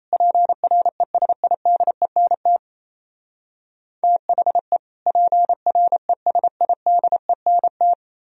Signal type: Morse code

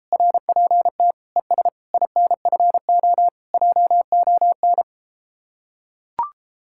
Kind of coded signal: Morse code

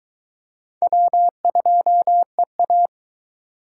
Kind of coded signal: Morse code

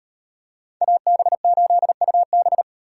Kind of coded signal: Morse code

